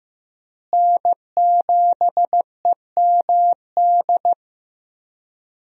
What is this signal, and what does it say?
Morse code, heard as a keyed tone, about 15 wpm, 700 Hz. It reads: N7EMD
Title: USA Call Signs - 5 Characters - Repeated 3x - 15wpm